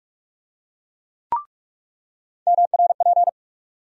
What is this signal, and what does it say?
Morse code, heard as a keyed tone, about 44 wpm, 700 Hz. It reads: MRP
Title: USA Call Sign Postfixes - 3 Letters 45wpm